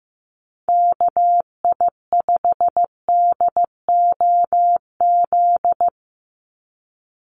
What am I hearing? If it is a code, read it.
Morse code, heard as a keyed tone, about 15 wpm, 700 Hz. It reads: KI5DOZ
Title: USA Call Signs 15wpm